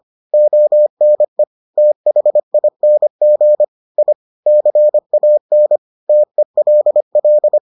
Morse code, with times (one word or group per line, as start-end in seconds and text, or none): 0.3-1.4 ONE
1.8-3.6 THING
4.0-4.1 I
4.5-5.8 CAN
6.1-7.6 TELL